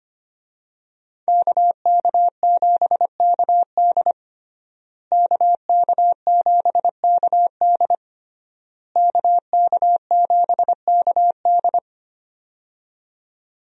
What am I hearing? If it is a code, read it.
Morse code, heard as a keyed tone, about 25 wpm, 700 Hz. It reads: KK7KD KK7KD KK7KD
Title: USA Call Signs - 5 Characters - Repeated 3x - 25wpm